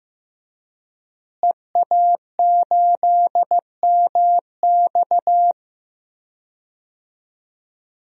Morse code, heard as a keyed tone, about 15 wpm, 700 Hz. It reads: EA8MX
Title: DXCC-100 Call Signs - 5 Characters - Repeated 3x - 15wpm